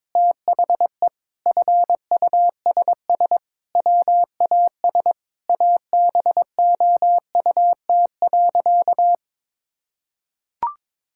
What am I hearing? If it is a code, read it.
Morse code, heard as a keyed tone, about 22 wpm, 700 Hz. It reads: THE FUSS WAS ABOUT.
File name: practice / 22wpm / Sentences with 8-Word Phrases - 22wpm